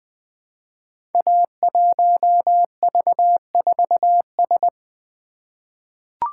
Morse code, heard as a keyed tone, about 20 wpm, 700 Hz. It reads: A1V4S
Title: Code Groups - 5 Letter:Numbers - 20wpm